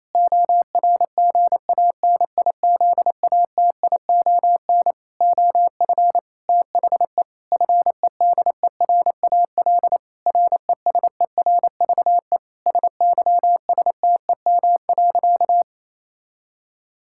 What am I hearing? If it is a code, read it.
Morse code, heard as a keyed tone, about 28 wpm, 700 Hz. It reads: ORGANIZATION OF THE FEDERAL RESERVE SYSTEM.
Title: Sentences from Top 1000 Words - Review Part 6 - 28wpm